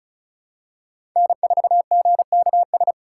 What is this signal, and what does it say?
Morse code, heard as a keyed tone, about 36 wpm, 700 Hz. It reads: N4GKS
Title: USA Call Signs - 5 Characters - 4x Character Spacing - 35wpm